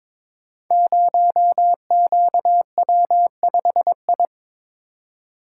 Morse code, heard as a keyed tone, about 21 wpm, 700 Hz. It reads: 0QW5I
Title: Code Groups - 5 Letter:Numbers - 22wpm